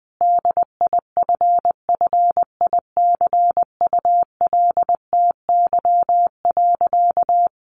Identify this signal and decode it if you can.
Morse code, heard as a keyed tone, about 19 wpm, 700 Hz. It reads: DIFFICULTY.
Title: Sentences from Top 1500 Words 20wpm